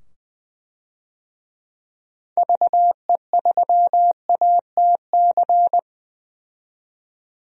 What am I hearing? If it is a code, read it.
Morse code, heard as a keyed tone, about 20 wpm, 700 Hz. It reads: VE3ATC
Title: DXCC-100 Call Signs - 6 Characters - 20wpm